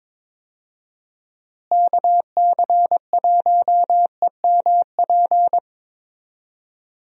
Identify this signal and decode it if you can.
Morse code, heard as a keyed tone, about 22 wpm, 700 Hz. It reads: KC1EMP
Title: USA Call Signs - 6 Characters - Repeated 2x - 22wpm